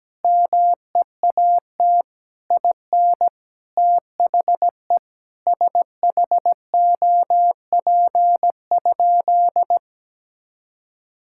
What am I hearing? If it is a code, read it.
Morse code, heard as a keyed tone, about 17 wpm, 700 Hz. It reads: MEAT IN THE SHOP?
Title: Sentences from Top 1600 Words 17wpm